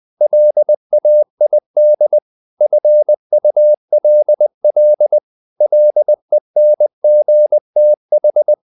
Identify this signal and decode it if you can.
Morse code, heard as a keyed tone, about 20 wpm, 600 Hz. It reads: LAID FULL LENGTH
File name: the old man and the sea / the old man and the sea part 1 20wpm